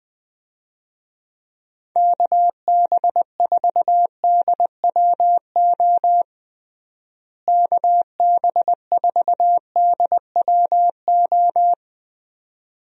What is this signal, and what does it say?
Morse code, heard as a keyed tone, about 20 wpm, 700 Hz. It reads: KB4DWO KB4DWO
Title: USA Call Signs - 6 Characters - Repeated 2x - 20wpm